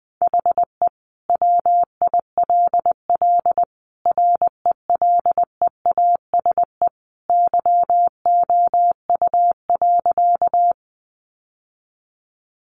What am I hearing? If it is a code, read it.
Morse code, heard as a keyed tone, about 20 wpm, 700 Hz. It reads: HE WILL RELEASE YOU.
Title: Sentences from Top 1100 Words 20wpm